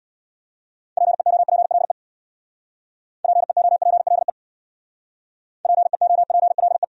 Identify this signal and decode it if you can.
Morse code, heard as a keyed tone, about 68 wpm, 700 Hz. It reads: PEOPLE PEOPLE PEOPLE